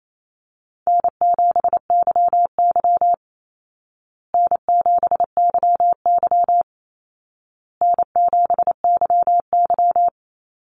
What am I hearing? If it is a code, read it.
Morse code, heard as a keyed tone, about 28 wpm, 700 Hz. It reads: N7YY N7YY N7YY
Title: Easy Call Signs Repeated 3x - 28wpm